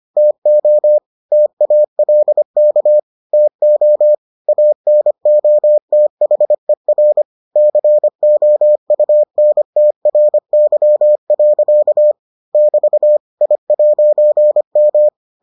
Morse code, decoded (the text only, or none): TO TALK TO ANOTHER COUNTRY. <BT> I'M